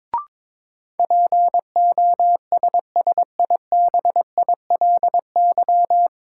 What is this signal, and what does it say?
Morse code, heard as a keyed tone, about 21 wpm, 700 Hz. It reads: POSSIBILY
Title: Sentences from Top 1300 Words - New Word Drill Alt - 22wpm